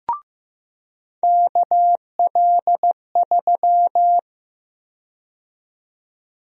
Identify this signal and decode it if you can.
Morse code, heard as a keyed tone, about 15 wpm, 700 Hz. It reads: KL3
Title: USA Call Sign Prefixes - 15wpm